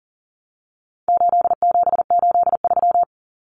Morse code, heard as a keyed tone, about 40 wpm, 700 Hz. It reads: 8783
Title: Common Cut Numbers - 4-Digit 40wpm